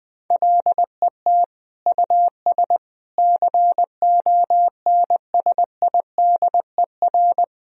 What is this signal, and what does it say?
Morse code, heard as a keyed tone, about 21 wpm, 700 Hz. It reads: LET US CONSIDER
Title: Sentences from Top 1000 Words - Review Part 4 - 20wpm